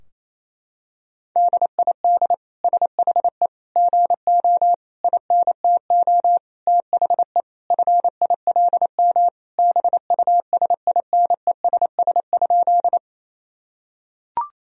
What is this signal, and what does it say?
Morse code, heard as a keyed tone, about 27 wpm, 700 Hz. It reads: DID SHE GO INTO THE FILM BUSINESS?
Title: Sentences from Top 1000 Words - Review Part 3 - 28wpm